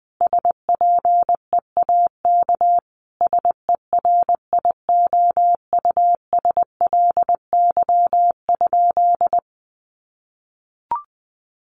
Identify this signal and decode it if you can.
Morse code, heard as a keyed tone, about 20 wpm, 700 Hz. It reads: SPEAK SERIOUSLY?
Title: Sentences from Top 1200 Words 20wpm